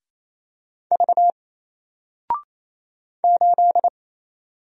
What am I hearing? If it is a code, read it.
Morse code, heard as a keyed tone, about 27 wpm, 700 Hz. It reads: V 8
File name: Single-Letter-Number-28wpm